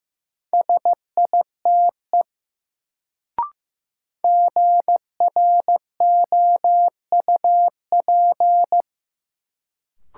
Morse code, heard as a keyed tone, about 15 wpm, 700 Hz. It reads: SITE GROUP
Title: Sentences from Top 1000 Words - Review Part 1 - New Word Drill - 15wpm